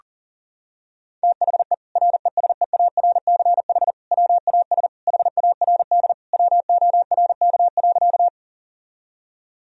Morse code, heard as a keyed tone, about 40 wpm, 690 Hz. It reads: THE RESEARCH WAS HARD WORK.
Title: Sentences from Top 1000 Words - Review Part 4 - 40wpm